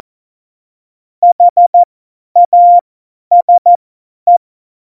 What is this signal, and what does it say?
Morse code, heard as a keyed tone, about 14 wpm, spaced as for 10 wpm, 700 Hz. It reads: HASE
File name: Morsekurs / Witze / Witz 03. November 2020, 50 F70